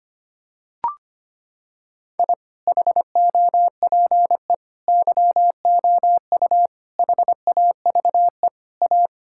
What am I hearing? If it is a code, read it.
Morse code, heard as a keyed tone, about 25 wpm, 700 Hz. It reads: I HOPE YOU HAVE A
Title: Sentences from Top 1000 Words - Review Part 6 - 25wpm